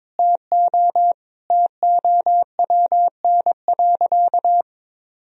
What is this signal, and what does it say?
Morse code, heard as a keyed tone, about 22 wpm, 700 Hz. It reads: TO TOWN.
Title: Sentences from Top 1000 Words - Review Part 1 - 22wpm